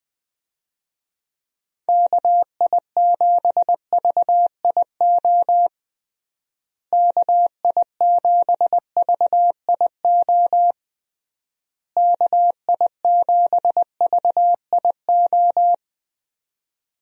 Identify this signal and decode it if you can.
Morse code, heard as a keyed tone, about 20 wpm, 700 Hz. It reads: KI7VIO KI7VIO KI7VIO
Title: USA Call Signs - 6 Characters - Repeated 3x - 20wpm